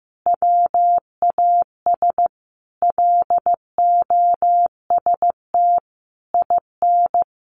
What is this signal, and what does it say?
Morse code, heard as a keyed tone, about 14 wpm, 700 Hz. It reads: WAS LOST IN
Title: A Sound Of Thunder 15wpm